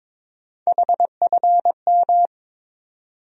Morse code, heard as a keyed tone, about 21 wpm, 700 Hz. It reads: HFM
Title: USA Call Sign Postfixes - 3 Letters 22wpm